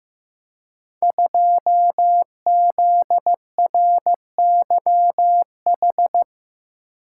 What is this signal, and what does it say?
Morse code, heard as a keyed tone, about 15 wpm, 700 Hz. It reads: 2ZRYH
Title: Code Groups - 5 Letter:Numbers - 15wpm